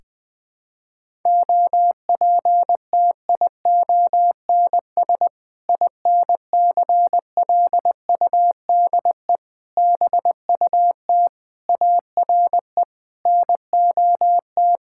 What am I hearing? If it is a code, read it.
Morse code, heard as a keyed tone, about 21 wpm, 700 Hz. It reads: OPTIONS INCLUDE BUT ARE NOT
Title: Sentences with 6-Word Phrases - 20wpm